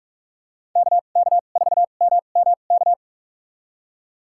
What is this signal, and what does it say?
Morse code, heard as a keyed tone, about 46 wpm, spaced as for 33 wpm, 690 Hz. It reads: KK4MMK
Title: USA Call Signs - 6 Characters - 2x Character Spacing - 45wpm